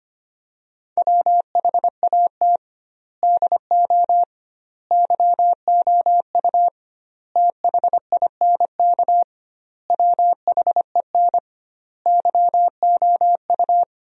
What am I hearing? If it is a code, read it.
Morse code, heard as a keyed tone, about 25 wpm, 700 Hz. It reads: WHAT DO YOU THINK WHEN YOU
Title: 2x Word-Spacing - Sentences from Top 300 Words - 25wpm